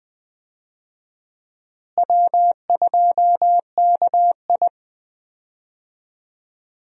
Morse code, heard as a keyed tone, about 20 wpm, 700 Hz. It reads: W2KI
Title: USA Call Signs - 4 Characters - Repeated 2x - 20wpm